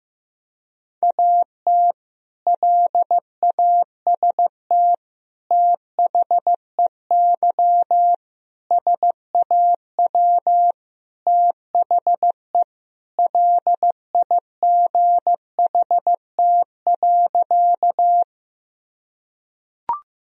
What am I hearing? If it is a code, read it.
Morse code, heard as a keyed tone, about 16 wpm, 700 Hz. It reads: AT LAST THEY SAW THE LIGHT.
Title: Sentences from Top 1000 Words - Review Part 1 - 15wpm